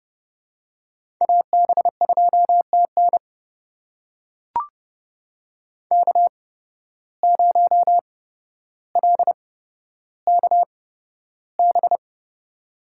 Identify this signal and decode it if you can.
Morse code, heard as a keyed tone, about 30 wpm, spaced as for 13 wpm, 700 Hz. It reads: AB2TN K0LKB
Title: USA Call Signs - 8x Character Spacing - 30wpm